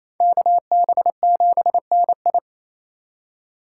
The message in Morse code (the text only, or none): KB7NI